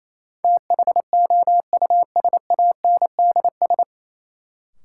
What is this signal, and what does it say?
Morse code, heard as a keyed tone, about 28 wpm, 700 Hz. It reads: THOUSANDS
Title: Sets of 4 Words - New Word Drill - 28wpm